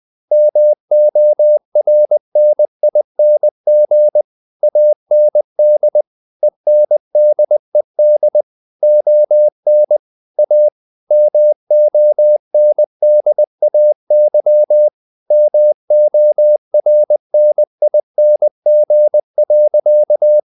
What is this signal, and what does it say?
Morse code, heard as a keyed tone, about 20 wpm, 600 Hz. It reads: MORNING AND ENDED ON A MONDAY MORNING.
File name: the old man and the sea part 2 20wpm